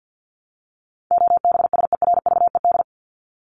Morse code, heard as a keyed tone, about 50 wpm, 700 Hz. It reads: OBSERVED
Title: ED Suffix - Top 100 Words with Base Words 50wpm